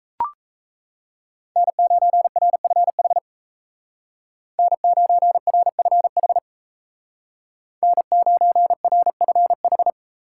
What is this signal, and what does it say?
Morse code, heard as a keyed tone, about 34 wpm, 700 Hz. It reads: N9RFH N9RFH N9RFH
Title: Speed Racing - USA Call Signs 28wpm